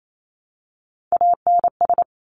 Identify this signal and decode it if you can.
Morse code, heard as a keyed tone, about 28 wpm, 700 Hz. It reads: ANS
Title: Top Three Letter Combinations in Words 28wpm